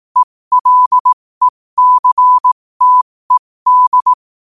Morse code, heard as a keyed tone, about 18 wpm, spaced as for 15 wpm, 1000 Hz. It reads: ELECTED
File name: qotdfiles / 15wpm